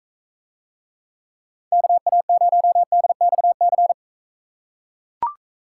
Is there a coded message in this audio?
Yes: Morse code from 1.7 to 3.9 s, heard as a keyed tone, about 42 wpm, 700 Hz. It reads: KA0DXC